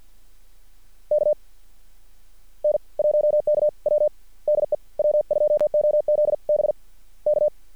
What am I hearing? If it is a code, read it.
Morse code, heard as a keyed tone, about 46 wpm, spaced as for 39 wpm, 600 Hz. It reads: K N1KW DE W2OZB K
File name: archived / CRSnet